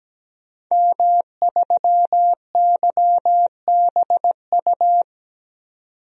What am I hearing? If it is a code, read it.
Morse code, heard as a keyed tone, about 17 wpm, 700 Hz. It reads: M3YBU
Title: DXCC-100 Call Signs - 5 Characters - 17wpm